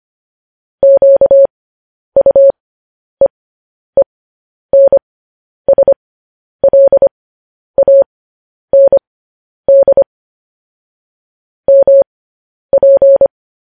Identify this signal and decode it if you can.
Morse code, heard as a keyed tone, about 25 wpm, spaced as for 10 wpm, 550 Hz. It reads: QUEENSLAND MP